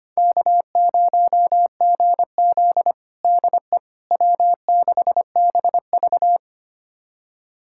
Morse code, heard as a keyed tone, about 25 wpm, 700 Hz. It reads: K0GZ DE W6BV
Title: USA Callsign DE Callsign - 4 Characters - 25wpm